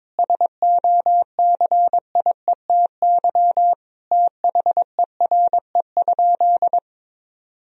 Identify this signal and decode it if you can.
Morse code, heard as a keyed tone, about 22 wpm, 700 Hz. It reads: SOCIETY THERE?
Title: Sentences from Top 1000 Words - Review Part 4 - 22wpm